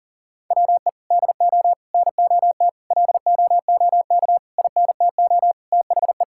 Morse code, heard as a keyed tone, about 40 wpm, 700 Hz. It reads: WE DO NOT LOOK INTO THE
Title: Sentences from Top 1600 Words 40wpm